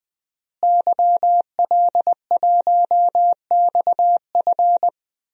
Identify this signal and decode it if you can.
Morse code, heard as a keyed tone, about 20 wpm, 700 Hz. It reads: YL1XF
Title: DXCC-100 Call Signs - 5 Characters - Repeated 3x - 20wpm